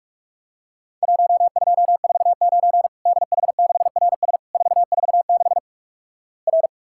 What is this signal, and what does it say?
Morse code, heard as a keyed tone, about 47 wpm, 700 Hz. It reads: 1249 DH6RS 446 R
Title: Worked All Europe DX Contest - QTC Exchange - 45wpm